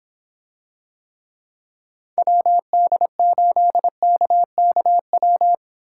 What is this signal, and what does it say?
Morse code, heard as a keyed tone, about 26 wpm, 700 Hz. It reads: WD8KKW